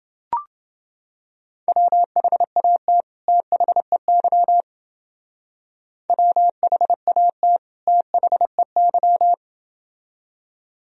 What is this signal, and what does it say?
Morse code, heard as a keyed tone, about 29 wpm, 700 Hz. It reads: WHAT THEY WHAT THEY